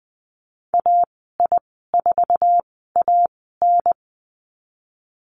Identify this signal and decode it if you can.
Morse code, heard as a keyed tone, about 20 wpm, spaced as for 15 wpm, 700 Hz. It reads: AI4AN
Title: 055 Lesson 10g - Letter r - Call Signs 20wpm